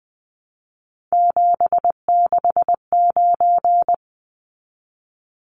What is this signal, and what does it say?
Morse code, heard as a keyed tone, about 19 wpm, 700 Hz. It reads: 769
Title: Numbers - 3-Digit 20wpm